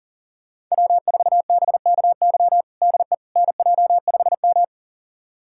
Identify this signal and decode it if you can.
Morse code, heard as a keyed tone, about 40 wpm, 700 Hz. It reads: W4BKY DE NJ5M
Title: USA Callsign DE Callsign 40wpm